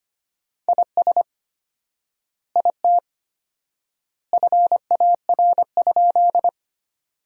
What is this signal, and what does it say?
Morse code, heard as a keyed tone, about 25 wpm, 700 Hz. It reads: IS IT FAR?
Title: 4x Word-Spacing - Sentences from Top 300 Words - 25wpm